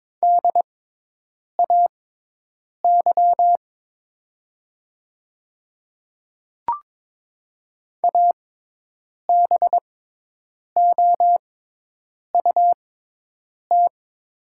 Morse code, heard as a keyed tone, about 22 wpm, spaced as for 8 wpm, 700 Hz. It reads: DAY ABOUT
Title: 6x Character Spacing - Top 100 Words - 22wpm